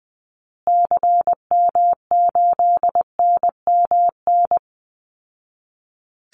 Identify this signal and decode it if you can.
Morse code, heard as a keyed tone, about 20 wpm, 700 Hz. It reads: CM8NMN
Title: Call Signs Repeated 2x - 20wpm